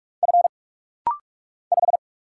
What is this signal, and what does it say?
Morse code, heard as a keyed tone, about 44 wpm, 700 Hz. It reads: F 5
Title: Single Letter-Number - Rapid-Fire 45wpm